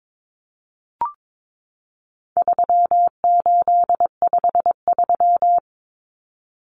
Morse code, heard as a keyed tone, about 22 wpm, 700 Hz. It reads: 3853
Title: Common Cut Numbers - 4-Digit 22wpm